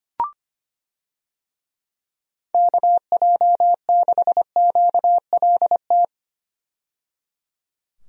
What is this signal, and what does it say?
Morse code, heard as a keyed tone, about 25 wpm, 700 Hz. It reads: KJ6QLT
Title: USA Call Signs - 6 Characters - 25wpm